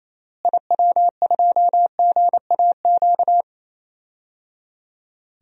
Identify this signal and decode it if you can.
Morse code, heard as a keyed tone, about 28 wpm, 700 Hz. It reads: IW2GAQ
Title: DXCC-100 Call Signs - 6 Characters - Repeated 3x - 28wpm